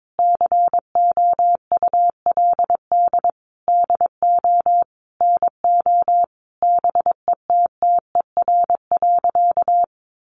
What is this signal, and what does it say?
Morse code, heard as a keyed tone, about 22 wpm, 700 Hz. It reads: COULD DO NO BETTER.
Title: Sentences from Top 1300 Words 22wpm